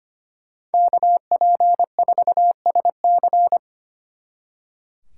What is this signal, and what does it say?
Morse code, heard as a keyed tone, about 30 wpm, 700 Hz. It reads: KP4SC
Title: Speed Racing - USA 2x2 Call Signs 25wpm